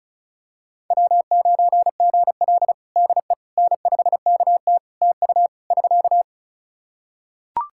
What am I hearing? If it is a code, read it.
Morse code, heard as a keyed tone, about 34 wpm, 700 Hz. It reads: W9GL DE N5KT TU <SK>
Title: Practice QSO - Exchange 4 - Ending - 35wpm